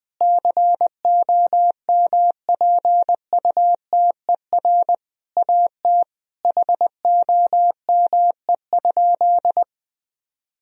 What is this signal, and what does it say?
Morse code, heard as a keyed tone, about 20 wpm, 700 Hz. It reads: COMPUTER AT HOME?
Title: Sentences from Top 1000 Words - Review Part 3 - 20wpm